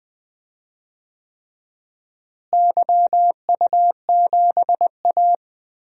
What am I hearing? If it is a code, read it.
Morse code, heard as a keyed tone, about 20 wpm, 700 Hz. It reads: YU7A
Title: DXCC-100 Call Signs - 4 Characters - Repeated 3x - 20wpm